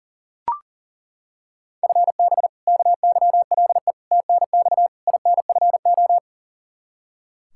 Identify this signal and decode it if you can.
Morse code, heard as a keyed tone, about 40 wpm, 700 Hz. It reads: FB KYLE TNX INFO
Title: Practice QSO - Exchange 3 - More Info - 40wpm